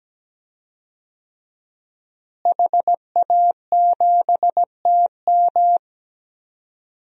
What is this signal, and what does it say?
Morse code, heard as a keyed tone, about 17 wpm, 700 Hz. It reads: HA7TM
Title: DXCC-100 Call Signs - 5 Characters - Repeated 2x - 17wpm